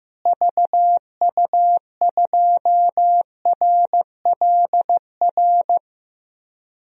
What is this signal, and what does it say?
Morse code, heard as a keyed tone, about 14 wpm, 700 Hz. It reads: VU2RLR
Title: DXCC-100 Call Signs - 6 Characters - Repeated 3x - 15wpm